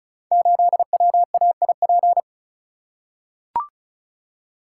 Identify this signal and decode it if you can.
Morse code, heard as a keyed tone, about 36 wpm, 700 Hz. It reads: 8WAIP
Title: Code Groups - 5 Letter:Numbers - 35wpm